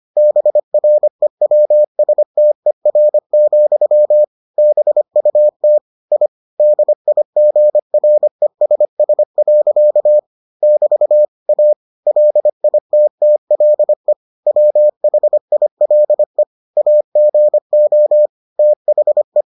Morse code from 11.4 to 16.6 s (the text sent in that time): A LITTLE WHILE